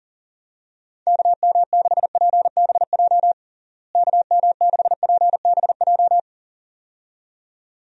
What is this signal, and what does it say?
Morse code, heard as a keyed tone, about 40 wpm, 700 Hz. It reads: KM6PBJ KM6PBJ
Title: USA Call Signs - 6 Characters - Repeated 2x - 40wpm